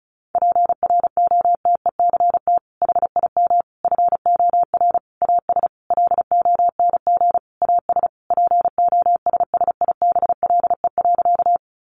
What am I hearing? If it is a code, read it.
Morse code, heard as a keyed tone, about 34 wpm, 700 Hz. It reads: PROTECT HIM FOR AS LONG AS POSSIBLE.
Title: Sentences from Top 900 Words 35wpm